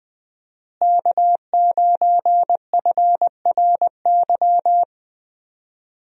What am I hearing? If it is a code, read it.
Morse code, heard as a keyed tone, about 20 wpm, 700 Hz. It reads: K9FRY
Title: USA Call Signs - 5 Characters - Repeated 3x - 20wpm